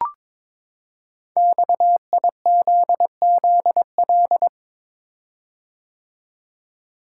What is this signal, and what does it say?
Morse code, heard as a keyed tone, about 22 wpm, 700 Hz. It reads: XIZZL
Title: Code Groups - 5 Letters - 22wpm